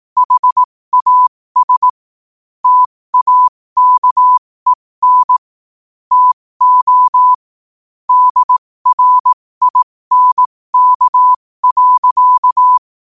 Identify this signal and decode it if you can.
Morse code, heard as a keyed tone, about 18 wpm, spaced as for 15 wpm, 1000 Hz. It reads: HAS TAKEN TO DRINK.
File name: qotdfiles / 15wpm